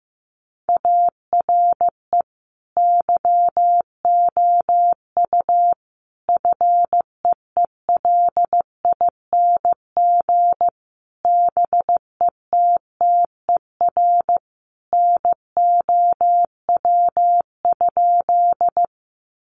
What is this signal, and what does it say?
Morse code, heard as a keyed tone, about 15 wpm, 700 Hz. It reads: ARE YOU FEELING BETTER NOW?
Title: Sentences from Top 1200 Words 15wpm